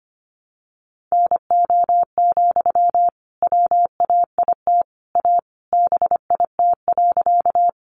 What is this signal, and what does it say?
Morse code, heard as a keyed tone, about 25 wpm, 700 Hz. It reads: NO, WAIT A BIT.
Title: Sentences from Top 900 Words 25wpm